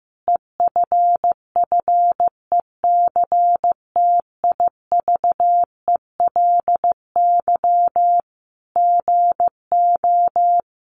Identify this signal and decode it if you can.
Morse code, heard as a keyed tone, about 15 wpm, 700 Hz. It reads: EFFECTIVELY GO
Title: Sentences from Top 1400 Words 15wpm